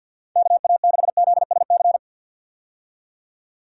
Morse code, heard as a keyed tone, about 51 wpm, 700 Hz. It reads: KA6ZSC
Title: USA Call Signs - 6 Characters - 50wpm